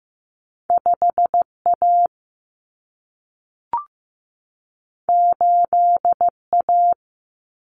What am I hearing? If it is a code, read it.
Morse code, heard as a keyed tone, about 15 wpm, 700 Hz. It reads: 5A 8A
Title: Common Cut Numbers - 2-Digit 15wpm